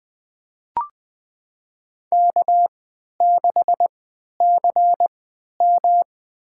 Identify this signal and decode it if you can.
Morse code, heard as a keyed tone, about 21 wpm, spaced as for 11 wpm, 700 Hz. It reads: K6CM
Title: USA Call Signs - 4 Characters - 3x Character Spacing - 20wpm